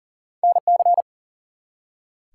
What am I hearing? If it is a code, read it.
Morse code, heard as a keyed tone, about 43 wpm, 700 Hz. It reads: NC
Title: US States and Canadian Provinces 40wpm